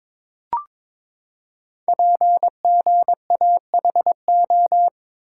Morse code, heard as a keyed tone, about 22 wpm, 700 Hz. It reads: PGAHO
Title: Code Groups - 5 Letters - 22wpm